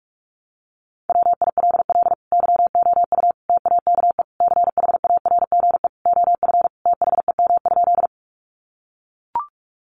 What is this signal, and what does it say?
Morse code, heard as a keyed tone, about 44 wpm, 700 Hz. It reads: WILL YOU TAKE CHARGE OF THEM?
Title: Sentences from Top 900 Words 45wpm